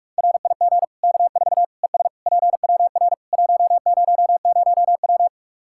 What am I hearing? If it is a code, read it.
Morse code, heard as a keyed tone, about 46 wpm, 700 Hz. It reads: RIG K4 ES PWR 100W
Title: K1USN SST QSO Practice 45wpm